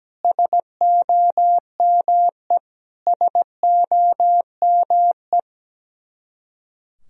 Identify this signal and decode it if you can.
Morse code, heard as a keyed tone, about 17 wpm, 700 Hz. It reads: SOME SOME
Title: Top 100 Words Repeated 17wpm